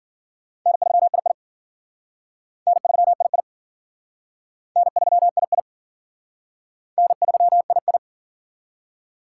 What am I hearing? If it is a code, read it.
Morse code, heard as a keyed tone, about 52 wpm, 700 Hz. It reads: N3II N3II N3II N3II